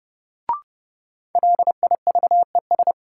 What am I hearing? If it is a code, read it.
Morse code, heard as a keyed tone, about 31 wpm, 700 Hz. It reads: LIVES
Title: Sets of 5 Words Encore - New Word Drill - 30wpm